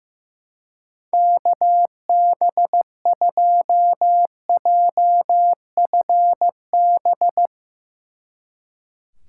Morse code, heard as a keyed tone, about 15 wpm, 700 Hz. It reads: KB2JFB
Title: Speed Racing - USA 2x3 Call Signs 15wpm